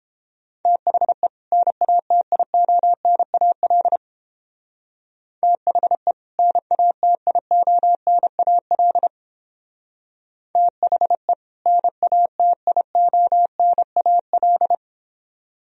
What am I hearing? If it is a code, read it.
Morse code, heard as a keyed tone, about 30 wpm, 700 Hz. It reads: THE NATIONAL THE NATIONAL THE NATIONAL